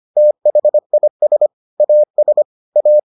Morse code, heard as a keyed tone, about 25 wpm, 600 Hz. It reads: THIS AS A